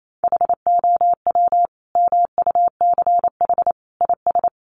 Morse code, heard as a keyed tone, about 28 wpm, 700 Hz. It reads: HOW MUCH IS
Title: Sentences from Top 1100 Words 28wpm